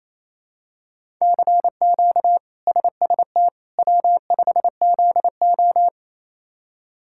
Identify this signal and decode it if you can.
Morse code, heard as a keyed tone, about 28 wpm, 700 Hz. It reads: CQ SST W5ZO
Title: K1USN SST QSO Practice - 28wpm